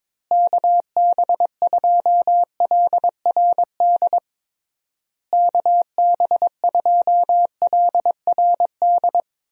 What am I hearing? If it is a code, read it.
Morse code, heard as a keyed tone, about 22 wpm, 700 Hz. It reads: KB2LRD KB2LRD
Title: USA Call Signs - 6 Characters - Repeated 3x - 22wpm